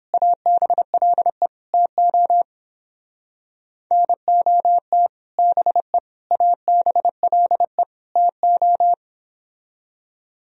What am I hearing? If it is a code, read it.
Morse code, heard as a keyed tone, about 31 wpm, 700 Hz. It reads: ABLE TO NOT BE ABLE TO